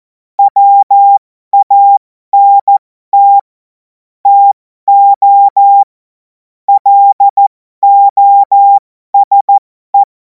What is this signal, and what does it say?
Morse code, heard as a keyed tone, about 14 wpm, spaced as for 12 wpm, 800 Hz. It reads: WANT TO LOSE
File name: Morsekurs / Witze